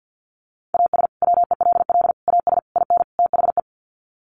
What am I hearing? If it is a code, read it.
Morse code, heard as a keyed tone, about 50 wpm, 700 Hz. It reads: AS WELL AS IN THE
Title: Sets of 5 Words Encore 50wpm